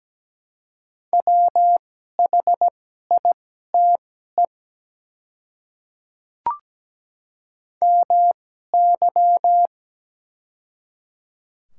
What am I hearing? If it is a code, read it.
Morse code, heard as a keyed tone, about 17 wpm, spaced as for 12 wpm, 690 Hz. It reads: WHITE MY
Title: 2x Character Spacing - Top 300 Words - 17wpm